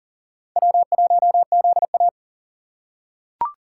Morse code, heard as a keyed tone, about 37 wpm, 700 Hz. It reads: W1ZA
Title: USA Call Signs - 4 Characters - 3x Character Spacing - 40wpm